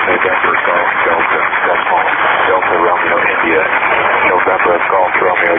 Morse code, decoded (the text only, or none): none